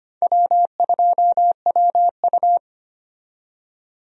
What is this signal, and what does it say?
Morse code, heard as a keyed tone, about 25 wpm, 700 Hz. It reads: W2WU
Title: USA Call Signs - 4 Characters - 25wpm